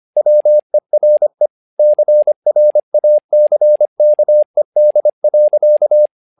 Morse code, heard as a keyed tone, about 25 wpm, 600 Hz. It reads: WERE CRACKED.
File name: learning center / tower trouble / tower trouble 25wpm